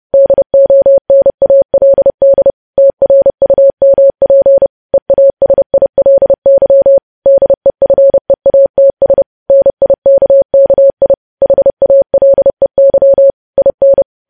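Morse code, heard as a keyed tone, about 30 wpm, 550 Hz. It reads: DONALD TRUMP EASILY DEFEATS NIKKI HALEY IN
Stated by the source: News headlines sent at 30 wpm.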